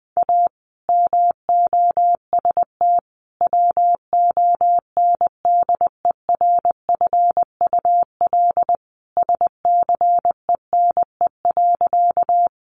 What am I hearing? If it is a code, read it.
Morse code, heard as a keyed tone, about 19 wpm, 700 Hz. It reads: A MOST WONDERFUL SCENE.
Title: Sentences from Top 1100 Words 20wpm